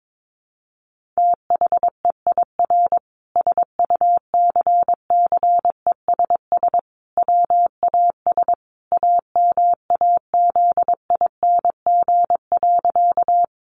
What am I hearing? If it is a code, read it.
Morse code, heard as a keyed tone, about 22 wpm, 700 Hz. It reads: THEIR SUCCESS WAS AMAZING.
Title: Sentences from Top 1500 Words 22wpm